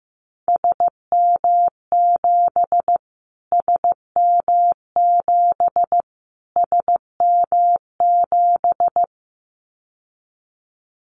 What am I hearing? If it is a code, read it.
Morse code, heard as a keyed tone, about 15 wpm, 700 Hz. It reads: SM7 SM7 SM7
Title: Callsign Prefixes for Top 25 DXCC - Repeated 3x - 15wpm